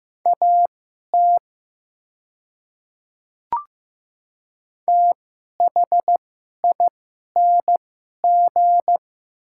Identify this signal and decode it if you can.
Morse code, heard as a keyed tone, about 15 wpm, spaced as for 11 wpm, 700 Hz. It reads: AT THING
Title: 2x Character Spacing - Top 100 Words - 15wpm